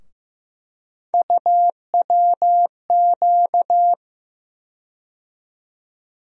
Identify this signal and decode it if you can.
Morse code, heard as a keyed tone, about 15 wpm, 700 Hz. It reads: UWQ
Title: USA Call Sign Postfixes - 3 Letters 15wpm